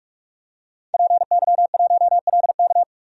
Morse code, heard as a keyed tone, about 46 wpm, 700 Hz. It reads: PY1LK
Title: DXCC-100 Call Signs - 5 Characters - 45wpm